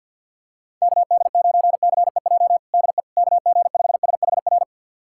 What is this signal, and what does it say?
Morse code, heard as a keyed tone, about 51 wpm, 690 Hz. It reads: KD9CEJ DE KG5SHR
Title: USA Callsign DE Callsign - 6 Characters - 50wpm